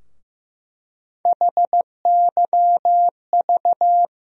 Morse code, heard as a keyed tone, about 15 wpm, 700 Hz. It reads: HYV
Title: USA Call Sign Postfixes - 3 Letters 15wpm